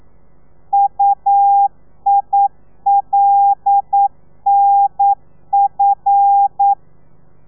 Morse code, heard as a keyed tone, about 9 wpm, 800 Hz. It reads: UILNF
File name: telegrafia / i2tfj / Lez5 40